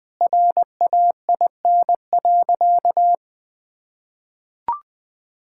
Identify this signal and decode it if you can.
Morse code, heard as a keyed tone, about 20 wpm, 700 Hz. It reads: RAIN.
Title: Sentences with 8-Word Phrases - 20wpm